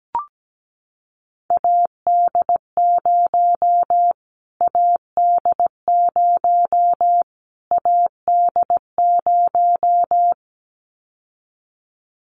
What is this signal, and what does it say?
Morse code, heard as a keyed tone, about 17 wpm, 700 Hz. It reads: AD0 AD0 AD0
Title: Callsign Prefixes for Top 25 DXCC - Repeated 3x - 17wpm